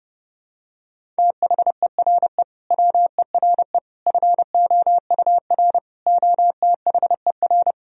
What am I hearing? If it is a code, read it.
Morse code, heard as a keyed tone, about 30 wpm, 700 Hz. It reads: THERE WERE FOUR OTHER
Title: Sentences from Top 1000 Words - Review Part 1 - 30wpm